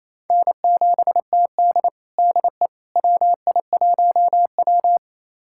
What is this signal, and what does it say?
Morse code, heard as a keyed tone, about 28 wpm, 700 Hz. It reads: N7TD DE WI1W
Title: USA Callsign DE Callsign - 4 Characters - 28wpm